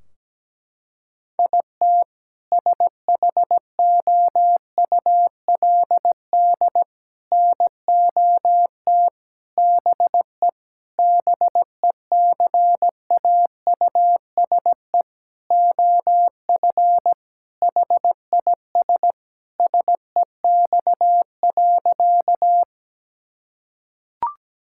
Morse code, heard as a keyed tone, about 17 wpm, 700 Hz. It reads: IT SHOULD NOT BE BECAUSE OF HIS SEX.
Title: Sentences from Top 1600 Words 17wpm